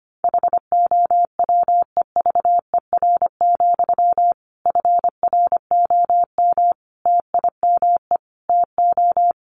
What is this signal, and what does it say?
Morse code, heard as a keyed tone, about 25 wpm, 700 Hz. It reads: HOWEVER, FROM TIME TO
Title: Sentences from Top 1300 Words 25wpm